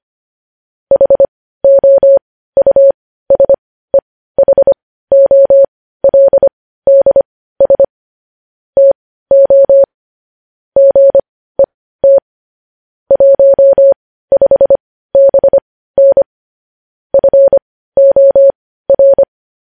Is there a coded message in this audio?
Yes: Morse code from 0.9 to 19.2 s, heard as a keyed tone, about 25 wpm, spaced as for 15 wpm, 550 Hz. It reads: HOUSEHOLDS TO GET 15BN FOR